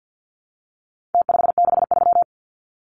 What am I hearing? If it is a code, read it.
Morse code, heard as a keyed tone, about 49 wpm, 700 Hz. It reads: T563
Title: Numbers - Contest Style - 4-Digit 50wpm